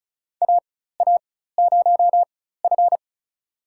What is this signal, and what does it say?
Morse code, heard as a keyed tone, about 36 wpm, spaced as for 16 wpm, 700 Hz. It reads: AA0F
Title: USA Call Signs - 4 Characters - 4x Character Spacing - 35wpm